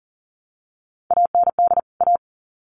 Morse code, heard as a keyed tone, about 40 wpm, 700 Hz. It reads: AND A
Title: Sets of 2 Words 40wpm